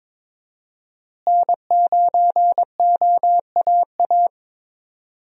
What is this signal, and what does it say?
Morse code, heard as a keyed tone, about 22 wpm, 700 Hz. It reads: N9OAA
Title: USA Call Signs - 5 Characters - Repeated 2x - 22wpm